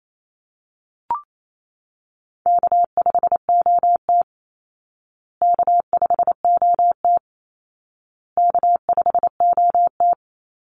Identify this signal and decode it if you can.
Morse code, heard as a keyed tone, about 28 wpm, 700 Hz. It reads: K5OT K5OT K5OT
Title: Call Signs Repeated 2x - 28wpm